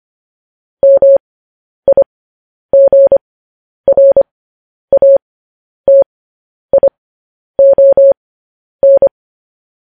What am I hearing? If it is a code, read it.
Morse code, heard as a keyed tone, about 25 wpm, spaced as for 10 wpm, 550 Hz. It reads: MIGRATION
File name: podcast / News headlines-1x-25x10